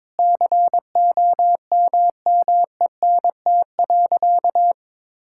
Morse code, heard as a keyed tone, about 22 wpm, 700 Hz. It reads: COMMENT.
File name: practice / 22wpm / Sentences with 8-Word Phrases - 22wpm